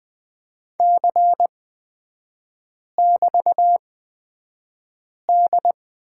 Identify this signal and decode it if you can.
Morse code, heard as a keyed tone, about 20 wpm, spaced as for 5 wpm, 700 Hz. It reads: C<BT>D
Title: 224 Lesson 36e - BT - Three Characters 20wpm